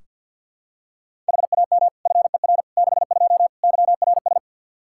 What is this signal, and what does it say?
Morse code, heard as a keyed tone, about 47 wpm, 700 Hz. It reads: HAM FER 62 YRS
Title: Practice QSO - Exchange 3 - More Info - 50wpm